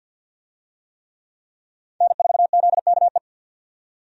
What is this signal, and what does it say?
Morse code, heard as a keyed tone, about 53 wpm, 700 Hz. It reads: N4ZKE
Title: USA Call Signs - 5 Characters - 50wpm